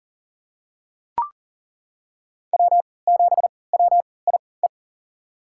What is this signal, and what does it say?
Morse code, heard as a keyed tone, about 40 wpm, spaced as for 23 wpm, 700 Hz. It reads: W7WIE
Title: USA Call Signs - 5 Characters - 3x Character Spacing - 40wpm